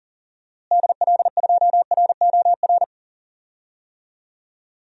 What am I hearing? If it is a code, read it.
Morse code, heard as a keyed tone, about 40 wpm, 700 Hz. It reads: DL2ROR
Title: DXCC-100 Call Signs - 6 Characters - Repeated 2x - 40wpm